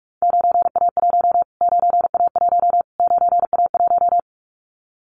Morse code, heard as a keyed tone, about 44 wpm, 700 Hz. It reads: 9A1 9A1 9A1
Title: Callsign Prefixes for Top 25 DXCC - Repeated 3x - 45wpm